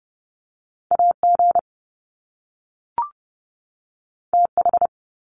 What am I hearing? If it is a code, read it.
Morse code, heard as a keyed tone, about 30 wpm, 700 Hz. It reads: AG TH
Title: Top Two Letter Combinations in Words 30wpm